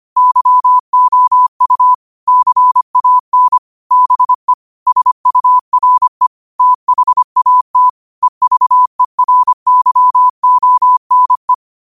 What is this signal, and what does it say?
Morse code, heard as a keyed tone, about 25 wpm, 1000 Hz. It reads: YOU CAN BE SURE THAT EVERYONE
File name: qotdfiles / 25wpm